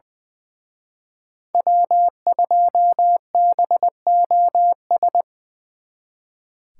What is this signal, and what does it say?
Morse code, heard as a keyed tone, about 23 wpm, 700 Hz. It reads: W2BOS
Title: Speed Racing - USA 1x3 Call Signs 20wpm